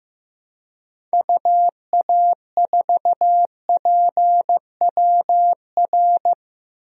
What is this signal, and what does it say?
Morse code, heard as a keyed tone, about 15 wpm, 700 Hz. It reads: UA4PWR
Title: DXCC-100 Call Signs - 6 Characters - 15wpm